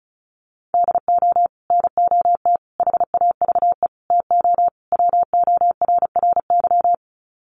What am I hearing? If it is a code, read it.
Morse code, heard as a keyed tone, about 34 wpm, 700 Hz. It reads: DO NOT HAVE TO WORRY
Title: Sets of 5 Words 35wpm